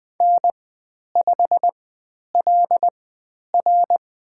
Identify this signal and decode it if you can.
Morse code, heard as a keyed tone, about 20 wpm, spaced as for 10 wpm, 700 Hz. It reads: N5LR
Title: 109 Lesson 18g - Letter m - Call Signs 20wpm